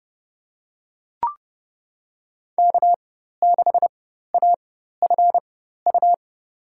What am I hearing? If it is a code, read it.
Morse code, heard as a keyed tone, about 30 wpm, spaced as for 14 wpm, 700 Hz. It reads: K6AFU
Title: USA Call Signs - 5 Characters - 4x Character Spacing - 30wpm